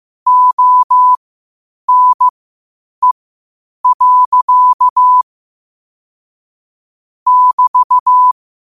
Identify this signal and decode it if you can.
Morse code, heard as a keyed tone, about 15 wpm, spaced as for 8 wpm, 1000 Hz. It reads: ONE. <BT>
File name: qotdfiles / 8wpm